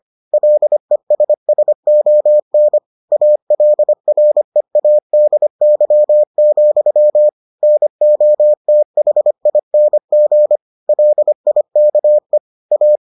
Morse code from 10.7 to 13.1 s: LIKE A

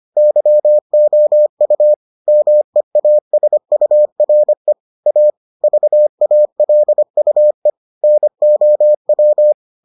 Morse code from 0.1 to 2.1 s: YOU

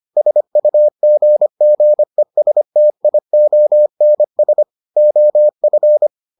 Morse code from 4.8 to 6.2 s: OF

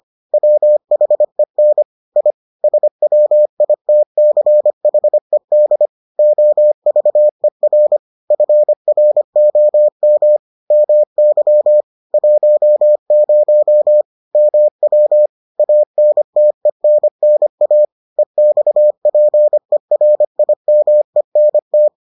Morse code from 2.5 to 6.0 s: SWITCHED

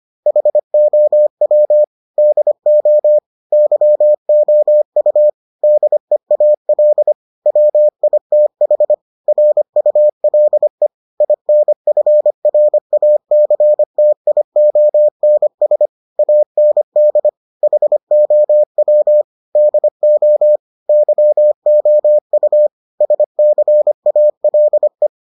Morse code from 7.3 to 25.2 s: WITH RULE INFRACTIONS AND HOW DO YOU SCALE

